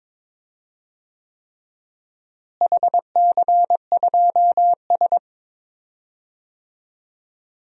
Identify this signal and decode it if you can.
Morse code, heard as a keyed tone, about 21 wpm, 700 Hz. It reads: HC2S
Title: DXCC-100 Call Signs - 4 Characters - Repeated 3x - 22wpm